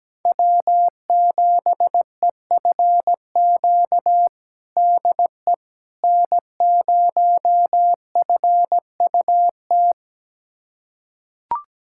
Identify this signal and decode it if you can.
Morse code, heard as a keyed tone, about 17 wpm, 700 Hz. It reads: W7EFQ DE N0FUT
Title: USA Callsign DE Callsign - 5 Characters - 17wpm